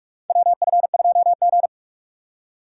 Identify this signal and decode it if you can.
Morse code, heard as a keyed tone, about 46 wpm, 700 Hz. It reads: WF2G
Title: USA Call Signs - 4 Characters - 2x Character Spacing - 45wpm